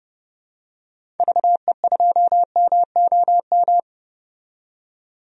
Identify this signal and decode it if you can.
Morse code, heard as a keyed tone, about 31 wpm, 700 Hz. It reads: VE2MOM
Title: DXCC-100 Call Signs - 6 Characters - 30wpm